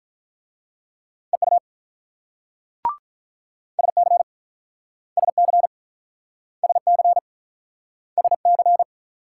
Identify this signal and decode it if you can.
Morse code, heard as a keyed tone, about 46 wpm, 700 Hz. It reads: EU SC SC SC SC